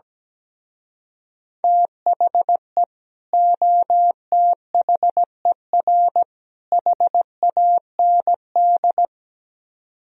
Morse code, heard as a keyed tone, about 17 wpm, 700 Hz. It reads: THE OTHER HAND
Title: Speed-Racing - Sets of 3 Words 17wpm